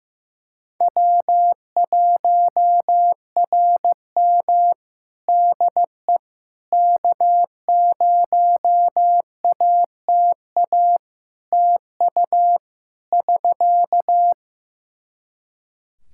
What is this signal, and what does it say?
Morse code, heard as a keyed tone, about 15 wpm, 700 Hz. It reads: W1RM DE K0ATA TU <SK>
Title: Practice QSO - Exchange 4 - Ending - 15wpm